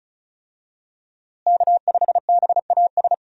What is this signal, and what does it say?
Morse code, heard as a keyed tone, about 36 wpm, 700 Hz. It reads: K5BAS
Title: USA Call Signs - 5 Characters - Repeated 3x - 35wpm